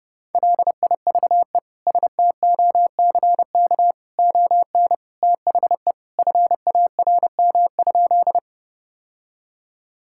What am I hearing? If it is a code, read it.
Morse code, heard as a keyed tone, about 30 wpm, 700 Hz. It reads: LIVE STOCK ON THE FARM?
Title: Sentences from Top 1000 Words - Review Part 5 - 30wpm